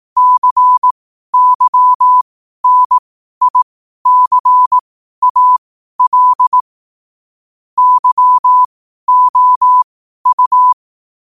Morse code, heard as a keyed tone, about 18 wpm, spaced as for 12 wpm, 1000 Hz. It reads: CYNICAL YOU
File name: qotdfiles / 12wpm